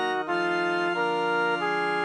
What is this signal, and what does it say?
There is no code.